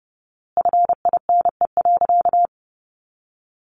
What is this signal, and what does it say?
Morse code, heard as a keyed tone, about 30 wpm, 700 Hz. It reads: FINE.
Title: A Sound Of Thunder 30wpm